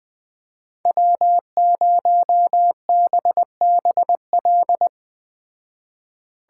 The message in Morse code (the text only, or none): W0BBL